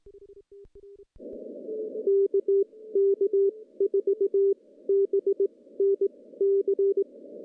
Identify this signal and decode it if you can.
Morse code, heard as a keyed tone, about 20 wpm, spaced as for 15 wpm, 410 Hz. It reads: KK4BNC
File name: contests / CW-SS-k5tr- -0200